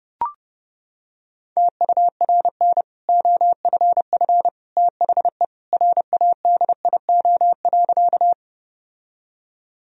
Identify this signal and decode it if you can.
Morse code, heard as a keyed tone, about 30 wpm, 700 Hz. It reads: TURN OFF THE RADIO.
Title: Sentences from Top 1000 Words - Review Part 3 - 30wpm